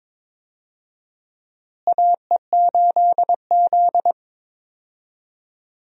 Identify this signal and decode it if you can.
Morse code, heard as a keyed tone, about 22 wpm, 700 Hz. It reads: AE8Z
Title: USA Call Signs - 4 Characters - Repeated 2x - 22wpm